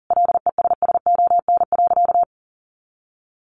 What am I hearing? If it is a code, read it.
Morse code, heard as a keyed tone, about 40 wpm, 700 Hz. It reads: LESSON.
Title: Sentences from Top 1200 Words 40wpm